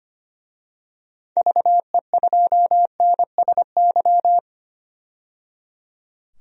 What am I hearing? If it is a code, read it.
Morse code, heard as a keyed tone, about 25 wpm, 700 Hz. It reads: VE2NSY
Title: DXCC-100 Call Signs - 6 Characters - 25wpm